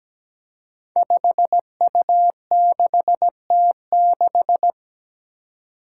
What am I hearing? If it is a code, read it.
Morse code, heard as a keyed tone, about 17 wpm, 700 Hz. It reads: 5U6T6
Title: Five Challenging Letter-Number Combinations 17wpm